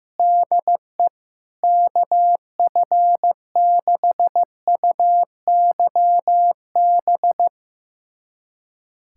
Morse code, heard as a keyed tone, about 15 wpm, 700 Hz. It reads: DE KF6UYB
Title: USA Callsign DE Callsign - 6 Characters - 15wpm